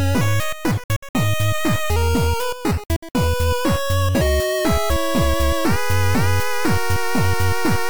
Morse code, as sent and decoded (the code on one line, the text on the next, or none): none